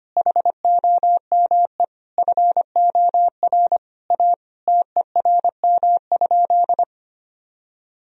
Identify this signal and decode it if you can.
Morse code, heard as a keyed tone, about 25 wpm, 700 Hz. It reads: HOME FOR A TERM?
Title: Sentences from Top 1000 Words - Review Part 4 - 25wpm